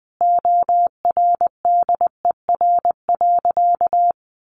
Morse code, heard as a keyed tone, about 20 wpm, 700 Hz. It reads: ORDER.
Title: Sentences from Top 900 Words 20wpm